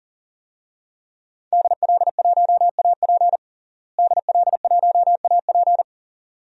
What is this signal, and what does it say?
Morse code, heard as a keyed tone, about 40 wpm, 700 Hz. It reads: DL1AP DL1AP
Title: DXCC-100 Call Signs - 5 Characters - Repeated 2x - 40wpm